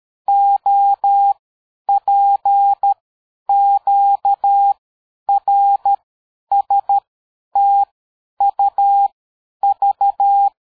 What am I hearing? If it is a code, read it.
Morse code, heard as a keyed tone, about 13 wpm, 800 Hz. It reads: O P Q R S T U V
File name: dk5ke / AbisZ 60BpM